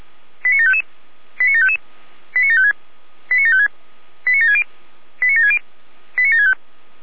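Five-tone selective call (ZVEI), heard as quick runs of five tones, tones of about 80 ms. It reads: 89755 89755 89756 89756 89760 89760 89765
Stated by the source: ZVEI five-tone selective call 89735, received by SDR.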